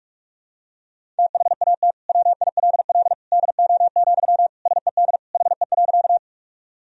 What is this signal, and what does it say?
Morse code, heard as a keyed tone, about 45 wpm, 700 Hz. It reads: THAT WILL DO, SED HE.
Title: Sentences from Top 100 Words and Abbreviations 45wpm